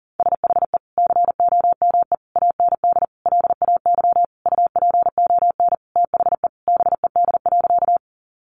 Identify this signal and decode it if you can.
Morse code, heard as a keyed tone, about 40 wpm, 700 Hz. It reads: SHE COME AND LAY UPON THE BED.
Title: Sentences from Top 1100 Words 40wpm